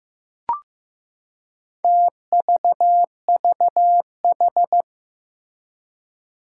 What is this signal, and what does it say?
Morse code, heard as a keyed tone, about 15 wpm, 700 Hz. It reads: TVVH
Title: Four Challenging Letter-Number Combinations 15wpm